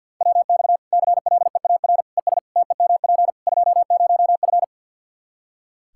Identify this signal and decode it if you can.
Morse code, heard as a keyed tone, about 49 wpm, 700 Hz. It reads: WX CLEAR ES TEMP 20F
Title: K1USN SST QSO Practice 50wpm